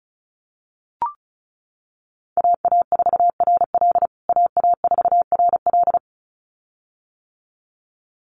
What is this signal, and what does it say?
Morse code, heard as a keyed tone, about 34 wpm, 700 Hz. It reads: AA4RL AA4RL
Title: USA Call Signs Repeated 2x 35wpm